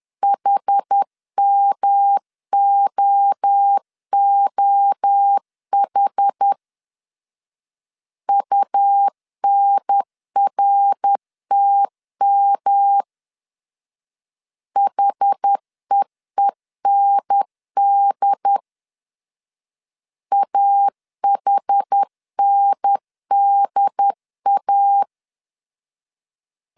Morse code, heard as a keyed tone, about 10 wpm, 800 Hz. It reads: HMOOH UNRTM HEEND AHNDA
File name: CW / DET / 1 3 1500C S C 10wpm 3335